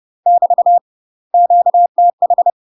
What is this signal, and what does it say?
Morse code, heard as a keyed tone, about 30 wpm, 700 Hz. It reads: <BT> QTH
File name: UK MASTER QSOs TRAINING / 30WPM / UK MASTER QSOs 0051